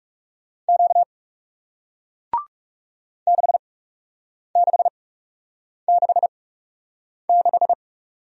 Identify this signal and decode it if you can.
Morse code, heard as a keyed tone, about 41 wpm, spaced as for 8 wpm, 700 Hz. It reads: Q 6666